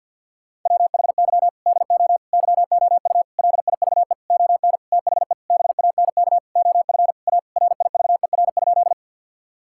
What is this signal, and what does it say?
Morse code, heard as a keyed tone, about 52 wpm, 700 Hz. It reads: WHY DO YOU LIVE ON THE BANK OF A RIVER?
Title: Sentences from Top 1000 Words - Review Part 6 - 50wpm